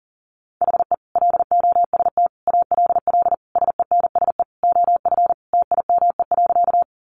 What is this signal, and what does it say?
Morse code, heard as a keyed tone, about 40 wpm, 700 Hz. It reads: HE LOST ALL SENSE OF TIME.
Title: Sentences from Top 1400 Words 40wpm